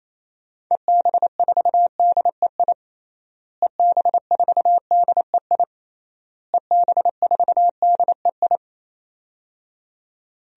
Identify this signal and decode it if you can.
Morse code, heard as a keyed tone, about 28 wpm, 700 Hz. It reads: EB4DEI EB4DEI EB4DEI
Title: DXCC-100 Call Signs - 6 Characters - Repeated 3x - 28wpm